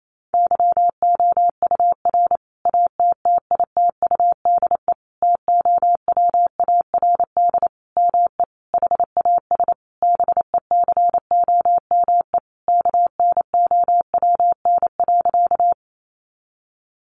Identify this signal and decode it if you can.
Morse code, heard as a keyed tone, about 28 wpm, 700 Hz. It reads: YOUR ATTITUDE TOWARD ME HAS BECOME KNOWN.
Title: Sentences from Top 1300 Words 28wpm